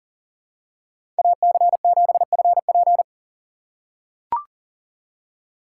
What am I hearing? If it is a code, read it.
Morse code, heard as a keyed tone, about 40 wpm, 700 Hz. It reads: AC7FP
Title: USA Call Signs - 5 Characters - 3x Character Spacing - 40wpm